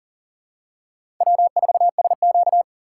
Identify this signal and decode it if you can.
Morse code, heard as a keyed tone, about 40 wpm, 700 Hz. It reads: W4SQ
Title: USA Call Signs - 6x Character Spacing - 40wpm